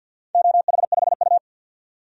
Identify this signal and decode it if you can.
Morse code, heard as a keyed tone, about 51 wpm, 700 Hz. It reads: OH5U
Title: DXCC-100 Call Signs - 4 Characters - 50wpm